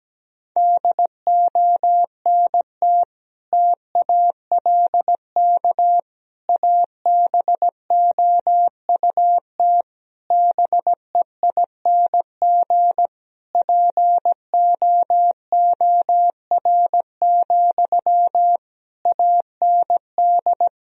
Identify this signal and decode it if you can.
Morse code, heard as a keyed tone, about 17 wpm, 700 Hz. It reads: DONT TALK ABOUT BEING POOR, AND
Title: Sentences from Top 1000 Words - Review Part 4 - 17wpm